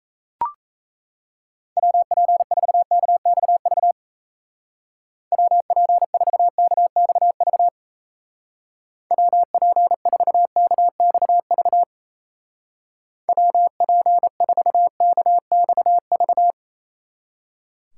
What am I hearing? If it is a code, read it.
Morse code, heard as a keyed tone, about 35 wpm, 700 Hz. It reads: WP4KXV WP4KXV WP4KXV WP4KXV